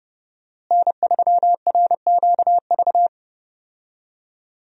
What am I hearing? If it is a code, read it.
Morse code, heard as a keyed tone, about 31 wpm, 700 Hz. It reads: N3RQV
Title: USA Call Signs - 5 Characters - Repeated 3x - 30wpm